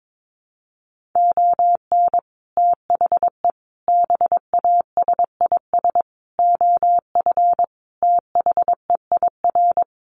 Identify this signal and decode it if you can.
Morse code, heard as a keyed tone, about 22 wpm, 700 Hz. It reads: ON THE BASIS OF THEIR
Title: Sets of 5 Words Encore 22wpm